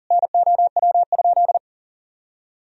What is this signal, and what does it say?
Morse code, heard as a keyed tone, about 40 wpm, 700 Hz. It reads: NOW?
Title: Sentences from Top 1000 Words - Review Part 2 - 40wpm